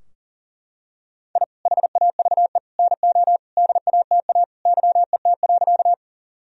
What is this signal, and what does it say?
Morse code, heard as a keyed tone, about 37 wpm, 700 Hz. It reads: I HAVE NO DATA YET.
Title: Sentences from Top 1000 Words - Review Part 3 - 40wpm